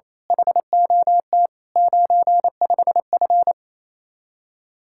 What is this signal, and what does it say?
Morse code, heard as a keyed tone, about 26 wpm, 700 Hz. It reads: HOT 95F
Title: Speed-Racing Common QSO Phrases - 28wpm